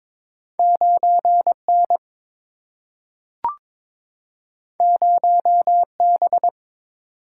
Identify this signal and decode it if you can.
Morse code, heard as a keyed tone, about 22 wpm, 700 Hz. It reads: 9 N 0 B
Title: Single-Letter Single-Number 22wpm